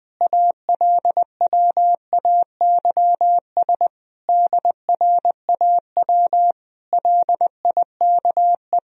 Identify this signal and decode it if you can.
Morse code, heard as a keyed tone, about 20 wpm, 700 Hz. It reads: ALWAYS DRAW LIKE
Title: Sentences from Top 1000 Words - Review Part 1 - 20wpm